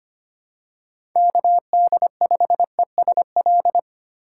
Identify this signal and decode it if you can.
Morse code, heard as a keyed tone, about 25 wpm, 700 Hz. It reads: KD5ESL
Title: USA Call Signs - 3x Character Spacing - 25wpm